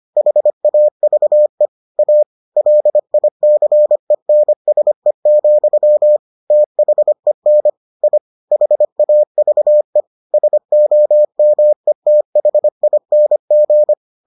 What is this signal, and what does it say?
Morse code, heard as a keyed tone, about 25 wpm, 600 Hz. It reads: HAVE A LICENSE, THEN I HAVE SOMETHING